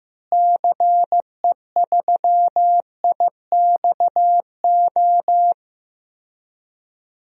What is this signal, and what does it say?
Morse code, heard as a keyed tone, about 15 wpm, 700 Hz. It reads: CE3IXO
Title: DXCC-100 Call Signs - 6 Characters - Repeated 2x - 15wpm